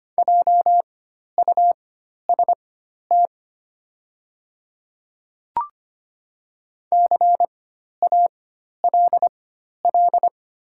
Morse code, heard as a keyed tone, about 25 wpm, spaced as for 12 wpm, 700 Hz. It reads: JUST CALL
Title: 4x Character Spacing - Top 300 Words - 25wpm